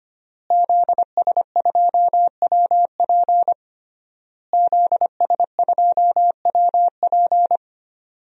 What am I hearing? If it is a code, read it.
Morse code, heard as a keyed tone, about 25 wpm, 700 Hz. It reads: ZS2WP ZS2WP
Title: DXCC-100 Call Signs - 5 Characters - Repeated 3x - 25wpm